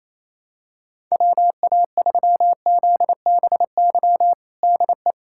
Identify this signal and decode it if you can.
Morse code, heard as a keyed tone, about 28 wpm, 700 Hz. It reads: WA3ZBY DE
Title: USA Callsign DE Callsign 28wpm